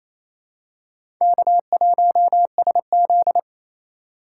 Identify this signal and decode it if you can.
Morse code, heard as a keyed tone, about 28 wpm, 700 Hz. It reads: K1SZ
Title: USA Call Signs - 4 Characters - 4x Character Spacing - 28wpm